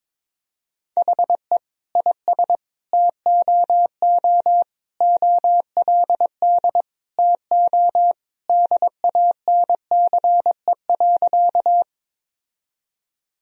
Morse code, heard as a keyed tone, about 22 wpm, 700 Hz. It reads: HE IS TOO OLD TO DANCE.
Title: Sentences from Top 1000 Words - Review Part 4 - 22wpm